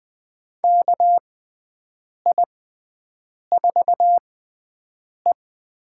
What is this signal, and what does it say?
Morse code, heard as a keyed tone, about 20 wpm, spaced as for 7 wpm, 700 Hz. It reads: KI4E
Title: USA Call Signs - 6 Characters - 6x Character Spacing - 20wpm